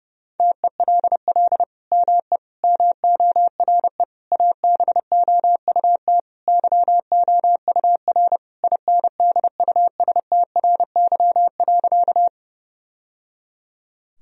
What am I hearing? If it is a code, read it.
Morse code, heard as a keyed tone, about 32 wpm, 700 Hz. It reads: TELL ME MORE ABOUT YOUR INDUSTRY.
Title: Sentences from Top 1000 Words - Review Part 4 - 30wpm